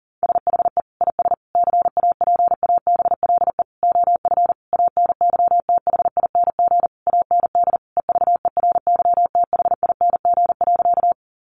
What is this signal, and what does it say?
Morse code, heard as a keyed tone, about 40 wpm, 700 Hz. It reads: SHE IS CAPABLE OF ANYTHING AND EVERYTHING.
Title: Sentences from Top 1200 Words 40wpm